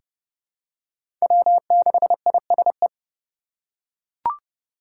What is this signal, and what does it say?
Morse code, heard as a keyed tone, about 30 wpm, 700 Hz. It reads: W6ISE
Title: USA Call Signs - 5 Characters - 4x Character Spacing - 30wpm